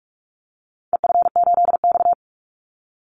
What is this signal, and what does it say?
Morse code, heard as a keyed tone, about 44 wpm, 700 Hz. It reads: EF8X
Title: Call Signs Repeated 2x - 45wpm